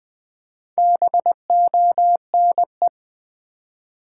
Morse code, heard as a keyed tone, about 20 wpm, 700 Hz. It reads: BONE
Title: Four Letter Words - 20wpm